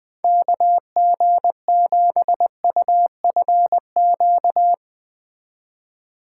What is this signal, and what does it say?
Morse code, heard as a keyed tone, about 20 wpm, 700 Hz. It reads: KG7UFQ
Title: Speed Racing - USA 2x3 Call Signs 17wpm